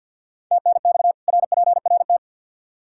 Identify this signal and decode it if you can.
Morse code, heard as a keyed tone, about 52 wpm, 700 Hz. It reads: TNX RPRT
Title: Practice QSO - Exchange 2 - Info - 50wpm